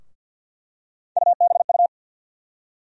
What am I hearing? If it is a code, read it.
Morse code, heard as a keyed tone, about 51 wpm, 700 Hz. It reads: UBU
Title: USA Call Sign Postfixes - 3 Letters 50wpm